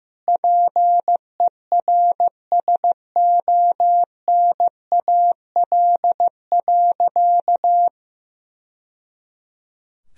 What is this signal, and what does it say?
Morse code, heard as a keyed tone, about 16 wpm, 700 Hz. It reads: PERSONAL.
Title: Sentences from Top 1000 Words - Review Part 4 - 15wpm